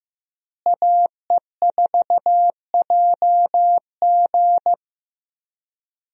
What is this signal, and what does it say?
Morse code, heard as a keyed tone, about 16 wpm, 700 Hz. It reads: AE4JG
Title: USA Call Signs - 5 Characters - Repeated 3x - 15wpm